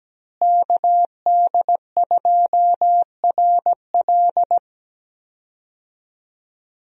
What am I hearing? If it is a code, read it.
Morse code, heard as a keyed tone, about 17 wpm, 700 Hz. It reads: KD2RL
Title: USA Call Signs - 5 Characters - Repeated 2x - 17wpm